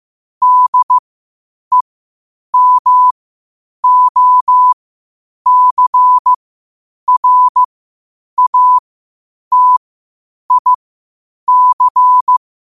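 Morse code, heard as a keyed tone, about 15 wpm, spaced as for 8 wpm, 1000 Hz. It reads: DEMOCRATIC
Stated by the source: Morse code at 8 wpm.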